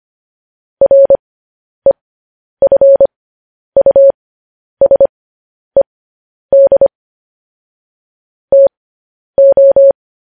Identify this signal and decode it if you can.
Morse code, heard as a keyed tone, about 25 wpm, spaced as for 10 wpm, 550 Hz. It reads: REFUSED TO